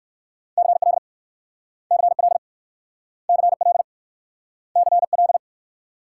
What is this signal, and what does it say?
Morse code, heard as a keyed tone, about 61 wpm, 690 Hz. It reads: CL CL CL CL